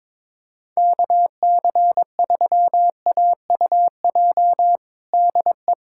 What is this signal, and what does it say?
Morse code, heard as a keyed tone, about 21 wpm, 700 Hz. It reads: KC3AUJ DE
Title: USA Callsign DE Callsign - 6 Characters - 22wpm